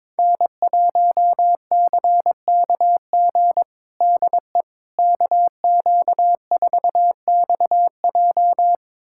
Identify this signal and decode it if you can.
Morse code, heard as a keyed tone, about 21 wpm, 700 Hz. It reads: N1CKG DE KQ4XJ
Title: USA Callsign DE Callsign - 5 Characters - 22wpm